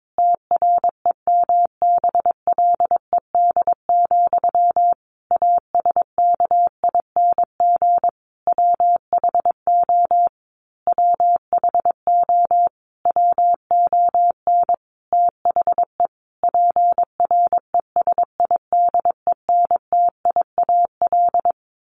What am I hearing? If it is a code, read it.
Morse code, heard as a keyed tone, about 22 wpm, 700 Hz. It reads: TREMBLED, ASKING WHO WHO WON THE PRESIDENTIAL
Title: A Sound Of Thunder 22wpm